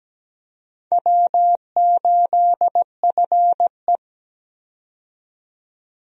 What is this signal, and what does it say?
Morse code, heard as a keyed tone, about 17 wpm, 700 Hz. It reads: W8FE
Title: USA Call Signs - 4 Characters - 17wpm